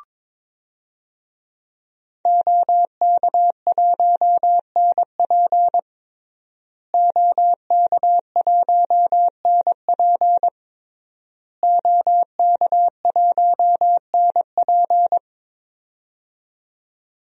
Morse code, heard as a keyed tone, about 23 wpm, 700 Hz. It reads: OK1NP OK1NP OK1NP
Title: DXCC-100 Call Signs - 5 Characters - Repeated 3x - 22wpm